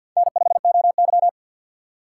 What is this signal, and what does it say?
Morse code, heard as a keyed tone, about 50 wpm, 700 Hz. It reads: N5OY
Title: USA Call Signs - 4x Character Spacing - 50wpm